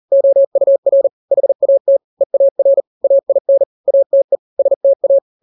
Morse code, heard as a keyed tone, about 40 wpm, 550 Hz. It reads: OUR HAT EAR AIN ATE STA
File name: n-grams / 3-grams 40 35 WPM